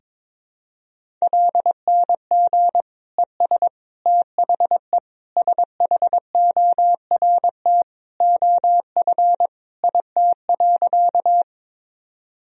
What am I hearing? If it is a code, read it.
Morse code, heard as a keyed tone, about 21 wpm, 700 Hz. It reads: LNG ES THE SHORT OF IT.
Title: Sentences from Top 400 Words and Abbreviations 22wpm